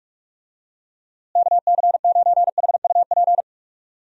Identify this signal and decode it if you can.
Morse code, heard as a keyed tone, about 45 wpm, 700 Hz. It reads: KC9HUP
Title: USA Call Signs - 6 Characters - Repeated 3x - 45wpm